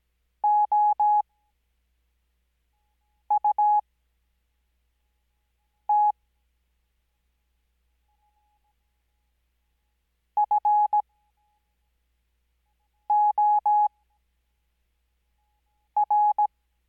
Morse code, heard as a keyed tone, about 17 wpm, spaced as for 4 wpm, 840 Hz. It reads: OUT FOR
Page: Farnsworth Spacing: 15WPM Character speed Spaced at 5WPM.